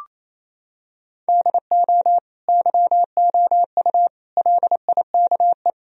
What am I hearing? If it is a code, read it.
Morse code, heard as a keyed tone, about 28 wpm, 700 Hz. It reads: DO YOU LIKE
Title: Sentences from Top 1000 Words - Review Part 4 - 28wpm